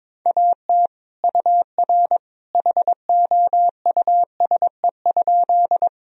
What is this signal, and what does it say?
Morse code, heard as a keyed tone, about 21 wpm, 700 Hz. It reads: AT UR HOUSE?
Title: Sentences from Top 200 Words and Abbreviations 22wpm